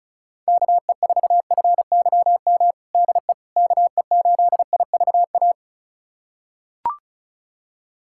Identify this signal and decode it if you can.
Morse code, heard as a keyed tone, about 34 wpm, 700 Hz. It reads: KE4FYM DE KE8IVA
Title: USA Callsign DE Callsign - 6 Characters - 35wpm